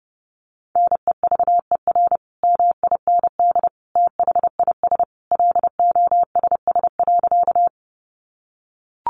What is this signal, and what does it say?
Morse code, heard as a keyed tone, about 28 wpm, 700 Hz. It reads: NEVER MIND THIS LOSS.
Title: Sentences from Top 1400 Words 30wpm